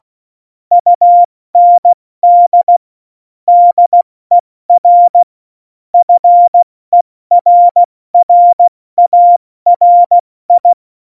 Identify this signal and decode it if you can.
Morse code, heard as a keyed tone, about 16 wpm, 700 Hz. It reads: UND DER FERRARI
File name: Morsekurs / Witze / Witz 04 August 2020, 70 F80